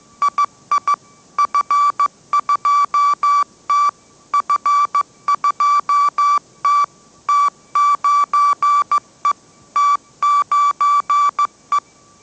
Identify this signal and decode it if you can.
Morse code, heard as a keyed tone, about 15 wpm, 1160 Hz. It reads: II F2T F2T T9E T9E